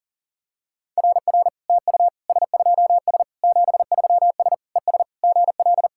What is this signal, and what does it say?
Morse code, heard as a keyed tone, about 37 wpm, 690 Hz. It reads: RR TU S2S 73S ES GL
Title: SOTA QSO Practice 40wpm